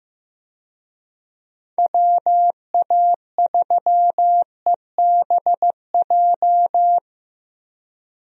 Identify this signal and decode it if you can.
Morse code, heard as a keyed tone, about 15 wpm, 700 Hz. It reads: WA3EBJ
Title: USA Call Signs - 6 Characters - Repeated 3x - 15wpm